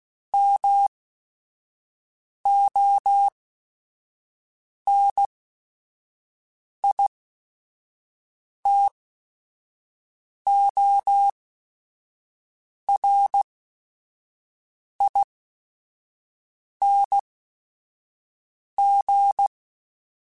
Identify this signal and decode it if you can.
Morse code, heard as a keyed tone, about 16 wpm, spaced as for 5 wpm, 800 Hz. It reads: MONITORING